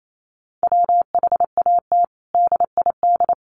Morse code, heard as a keyed tone, about 28 wpm, 700 Hz. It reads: WHAT DID
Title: Sets of 5 Words Encore2 28wpm